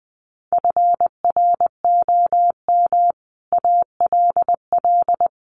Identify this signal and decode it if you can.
Morse code, heard as a keyed tone, about 20 wpm, 700 Hz. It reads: FROM ALL
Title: Sets of 5 Words Encore 20wpm